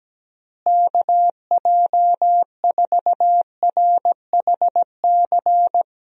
Morse code, heard as a keyed tone, about 17 wpm, 700 Hz. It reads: KJ4RHC
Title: USA Call Signs - 6 Characters - Repeated 2x - 17wpm